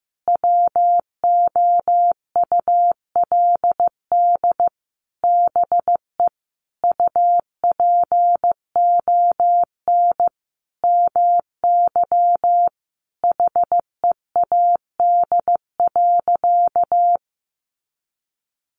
Morse code, heard as a keyed tone, about 15 wpm, 700 Hz. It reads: WOULD BE UPON MY HEAD.
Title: Sentences from Top 1100 Words 15wpm